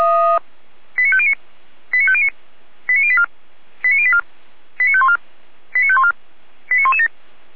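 Five-tone selective call (ZVEI), heard as quick runs of five tones, tones of about 70 ms. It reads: 89509 89509 89084 89084 89514 89514 89117
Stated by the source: ZVEI five-tone call received on SDR.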